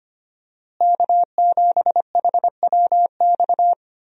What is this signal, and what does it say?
Morse code, heard as a keyed tone, about 25 wpm, 700 Hz. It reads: K7HWX
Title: USA Call Signs - 5 Characters - Repeated 3x - 25wpm